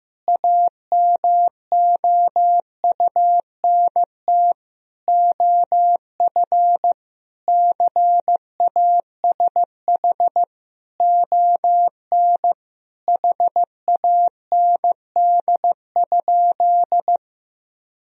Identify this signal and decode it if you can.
Morse code, heard as a keyed tone, about 15 wpm, 700 Hz. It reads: AMOUNT OF CASH ON HAND?
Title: Sentences from Top 1000 Words - Review Part 4 - 15wpm